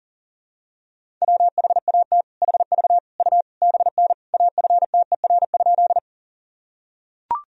Morse code, heard as a keyed tone, about 40 wpm, 700 Hz. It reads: WHAT HV U BN AFTER?
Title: Sentences from Top 200 Words and Abbreviations 40wpm